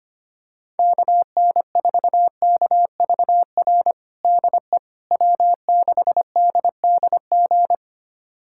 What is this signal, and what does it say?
Morse code, heard as a keyed tone, about 25 wpm, 700 Hz. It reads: KN4KVR DE W6DDG
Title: USA Callsign DE Callsign 25wpm